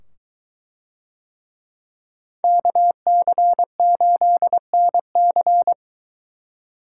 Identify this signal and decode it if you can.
Morse code, heard as a keyed tone, about 22 wpm, 700 Hz. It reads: KC8NC